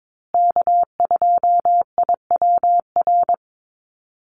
Morse code, heard as a keyed tone, about 22 wpm, 700 Hz. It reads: K2IWR
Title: USA Call Signs Repeated 3x 22wpm